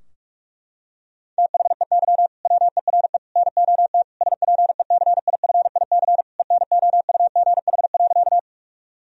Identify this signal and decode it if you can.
Morse code, heard as a keyed tone, about 42 wpm, 700 Hz. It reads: THEY WERE NOT SPECIFIC ENOUGH.
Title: Sentences from Top 1000 Words - Review Part 3 - 45wpm